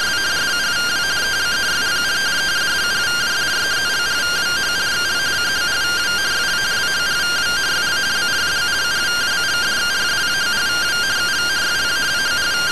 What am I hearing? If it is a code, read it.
Two-tone teleprinter signal, about 170 Hz shift.